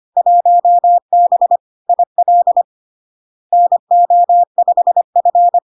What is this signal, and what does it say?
Morse code, heard as a keyed tone, about 25 wpm, 700 Hz. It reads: 1B IL NO5F
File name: LARK-FD-2022-Log-in-CW-25wpm-700-Hz